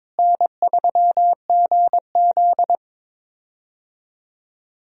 Morse code, heard as a keyed tone, about 22 wpm, 700 Hz. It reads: N3GZ
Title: USA Call Signs - 4 Characters - Repeated 3x - 22wpm